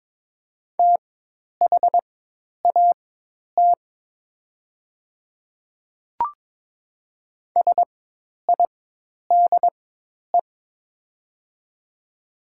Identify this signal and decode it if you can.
Morse code, heard as a keyed tone, about 22 wpm, spaced as for 10 wpm, 700 Hz. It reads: THAT SIDE
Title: 4x Character Spacing - Top 100 Words - 22wpm